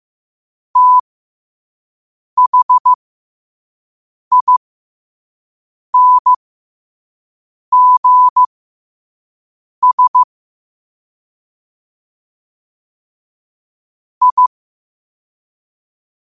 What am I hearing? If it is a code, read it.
Morse code, heard as a keyed tone, about 15 wpm, spaced as for 5 wpm, 1000 Hz. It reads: THINGS I
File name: qotdfiles / 5wpm